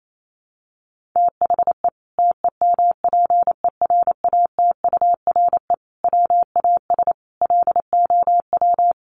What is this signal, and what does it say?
Morse code, heard as a keyed tone, about 28 wpm, 700 Hz. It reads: THE TEMPERATURE WAS LOW
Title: Sentences from Top 1500 Words 28wpm